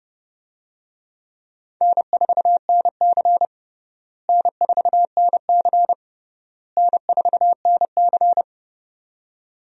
Morse code, heard as a keyed tone, about 30 wpm, 700 Hz. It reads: N4NC N4NC N4NC
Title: USA Call Signs - 4 Characters - Repeated 3x - 30wpm